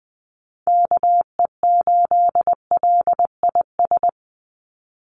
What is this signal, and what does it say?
Morse code, heard as a keyed tone, about 20 wpm, 700 Hz. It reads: KE8LIS
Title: USA Call Signs 20wpm